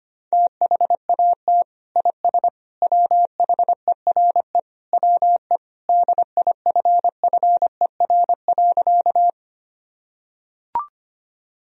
Morse code, heard as a keyed tone, about 25 wpm, 700 Hz. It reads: THAT IS WHERE WE DIFFER.
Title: Sentences from Top 1000 Words - Review Part 1 - 25wpm